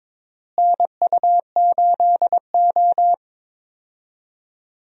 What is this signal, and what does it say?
Morse code, heard as a keyed tone, about 22 wpm, 700 Hz. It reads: NU8O
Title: USA Call Signs - 4 Characters - Repeated 3x - 22wpm